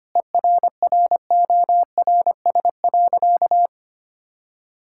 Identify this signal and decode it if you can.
Morse code, heard as a keyed tone, about 25 wpm, 700 Hz. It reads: ERRORS.
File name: practice / 25wpm / Difficult Sentences with Hints - 25wpm